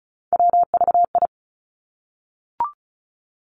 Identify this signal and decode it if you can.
Morse code, heard as a keyed tone, about 36 wpm, 700 Hz. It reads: WVI
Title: Three Challenging Letter-Number Combinations 35wpm